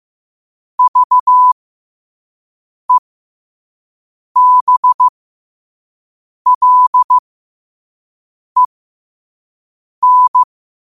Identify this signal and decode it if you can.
Morse code, heard as a keyed tone, about 15 wpm, spaced as for 5 wpm, 1000 Hz. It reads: VEBLEN
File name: qotdfiles / 5wpm